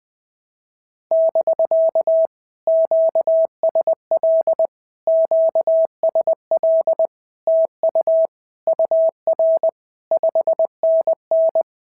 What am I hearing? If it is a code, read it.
Morse code, heard as a keyed tone, about 20 wpm, 650 Hz. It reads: <BK> QSL QSL TU UR 5NN
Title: POTA QSO Practice - 20wpm